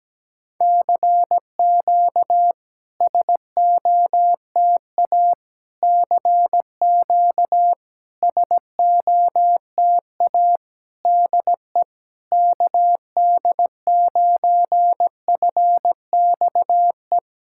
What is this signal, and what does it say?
Morse code, heard as a keyed tone, about 17 wpm, 690 Hz. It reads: CQ SOTA CQ SOTA DE KD9FXE
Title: SOTA QSO Practice 17wpm